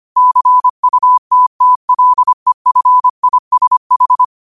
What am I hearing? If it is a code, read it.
Morse code, heard as a keyed tone, about 25 wpm, 1000 Hz. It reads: CUTTLEFISH
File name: 25wpm